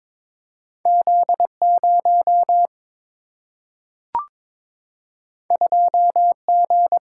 Morse code, heard as a keyed tone, about 22 wpm, 700 Hz. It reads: Z0 2G
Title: Single-Letter Single-Number 22wpm